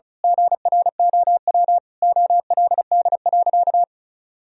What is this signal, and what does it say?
Morse code, heard as a keyed tone, about 34 wpm, 700 Hz. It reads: GROW OLD.
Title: Sentences from Top 1000 Words - Review Part 1 - 35wpm